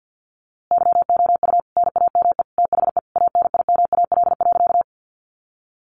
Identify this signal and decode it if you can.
Morse code, heard as a keyed tone, about 50 wpm, 700 Hz. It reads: YOU NAME THE ANIMAL.
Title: A Sound Of Thunder 50wpm